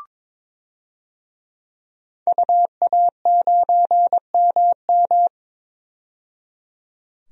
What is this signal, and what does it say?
Morse code, heard as a keyed tone, about 22 wpm, 700 Hz. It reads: UA9MM
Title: DXCC-100 Call Signs - 5 Characters - 22wpm